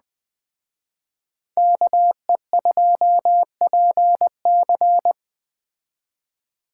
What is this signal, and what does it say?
Morse code, heard as a keyed tone, about 19 wpm, 700 Hz. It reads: KE2PC